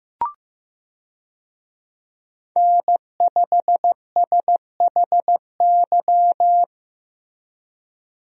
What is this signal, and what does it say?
Morse code, heard as a keyed tone, about 15 wpm, 700 Hz. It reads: N5SHY
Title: USA Call Signs - 5 Characters - Repeated 2x - 15wpm